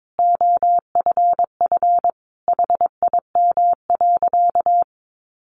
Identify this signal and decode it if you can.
Morse code, heard as a keyed tone, about 22 wpm, 700 Hz. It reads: OFF HIM.
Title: Sentences from Top 1300 Words 22wpm